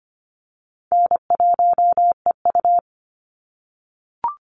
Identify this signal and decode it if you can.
Morse code, heard as a keyed tone, about 25 wpm, 700 Hz. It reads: N1EU
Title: NA SOTA Chaser Callsigns 25wpm